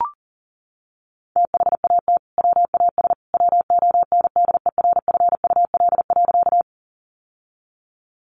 Morse code, heard as a keyed tone, about 40 wpm, 700 Hz. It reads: THAT WAS WONDERFUL.
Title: Sentences from Top 1100 Words 40wpm